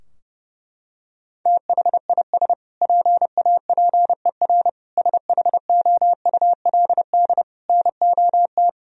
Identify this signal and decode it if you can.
Morse code, heard as a keyed tone, about 30 wpm, 700 Hz. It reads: THIS PAPER SHOULD NOT
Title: Sentences from Top 1000 Words - Review Part 3 - 30wpm